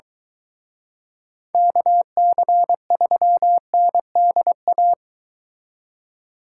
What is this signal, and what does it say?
Morse code, heard as a keyed tone, about 22 wpm, 700 Hz. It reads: KC3NDA